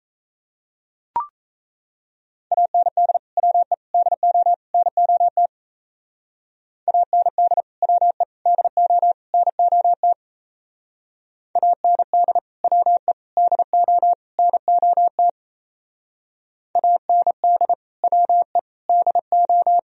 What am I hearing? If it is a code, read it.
Morse code, heard as a keyed tone, about 36 wpm, 700 Hz. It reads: AND WE DO NOT AND WE DO NOT AND WE DO NOT AND WE DO